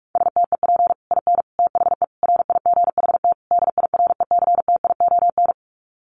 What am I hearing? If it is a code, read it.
Morse code, heard as a keyed tone, about 44 wpm, 700 Hz. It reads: STEP IN THE RIGHT DIRECTION
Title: Sets of 5 Words Encore 45wpm